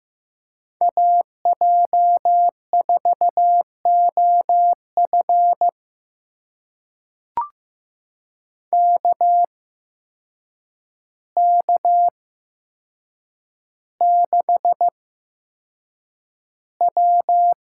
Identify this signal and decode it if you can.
Morse code, heard as a keyed tone, about 15 wpm, spaced as for 7 wpm, 700 Hz. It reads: AJ4OF KK6W
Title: USA Call Signs - 8x Character Spacing - 15wpm